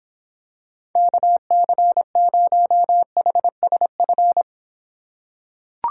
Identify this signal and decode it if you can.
Morse code, heard as a keyed tone, about 26 wpm, 700 Hz. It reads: KC0HSF